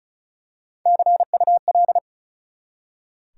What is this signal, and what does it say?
Morse code, heard as a keyed tone, about 34 wpm, 700 Hz. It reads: CUL
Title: Top 180 QSO Elements 35wpm